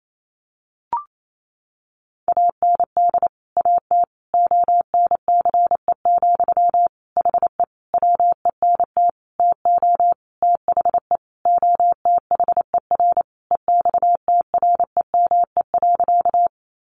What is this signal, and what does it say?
Morse code, heard as a keyed tone, about 28 wpm, 700 Hz. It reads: AND AT ONCE, HE WENT TO THE OTHER EXTREME.
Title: Sentences from Top 1300 Words 28wpm